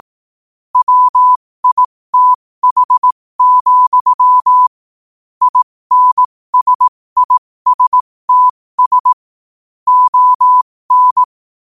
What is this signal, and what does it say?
Morse code, heard as a keyed tone, about 18 wpm, spaced as for 15 wpm, 1000 Hz. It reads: WITH, INSISTS ON
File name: qotdfiles / 15wpm